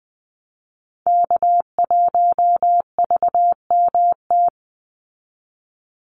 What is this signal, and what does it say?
Morse code, heard as a keyed tone, about 20 wpm, 700 Hz. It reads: K1VMT
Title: Call Signs Repeated 2x - 20wpm